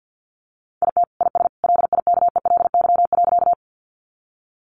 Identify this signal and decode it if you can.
Morse code, heard as a keyed tone, about 50 wpm, 700 Hz. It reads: IT IS LIKELY.
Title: Sentences from Top 600 Words 50wpm